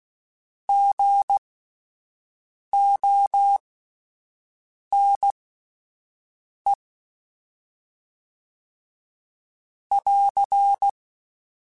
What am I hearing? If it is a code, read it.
Morse code, heard as a keyed tone, about 16 wpm, spaced as for 5 wpm, 800 Hz. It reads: GONE <AR>